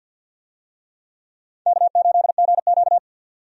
Morse code, heard as a keyed tone, about 52 wpm, 700 Hz. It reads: K8GQ
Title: USA Call Signs - 4 Characters - 50wpm